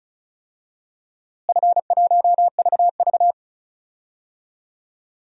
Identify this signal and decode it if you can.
Morse code, heard as a keyed tone, about 34 wpm, 700 Hz. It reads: F1VV
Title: DXCC-100 Call Signs - 4 Characters - Repeated 3x - 35wpm